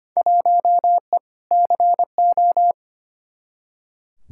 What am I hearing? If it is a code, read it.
Morse code, heard as a keyed tone, about 25 wpm, 700 Hz. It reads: 1E CO
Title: Field Day Exchanges - 25wpm